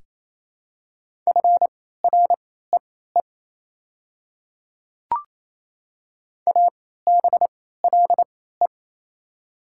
Morse code, heard as a keyed tone, about 28 wpm, spaced as for 16 wpm, 690 Hz. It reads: FREE ABLE
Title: 3x Character Spacing - Top 500 Words - 28wpm